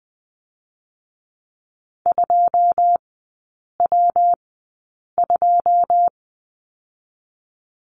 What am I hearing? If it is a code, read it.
Morse code, heard as a keyed tone, about 20 wpm, 700 Hz. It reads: 2 W 2
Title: 133 Lesson 22e - Question Mark - Three Characters 20wpm